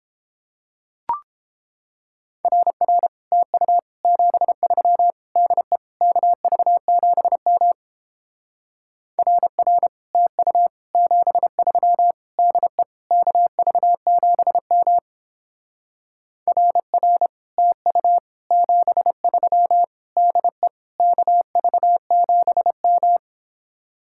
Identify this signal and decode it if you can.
Morse code, heard as a keyed tone, about 29 wpm, 700 Hz. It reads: RR TU 73 DE KV7M RR TU 73 DE KV7M RR TU 73 DE KV7M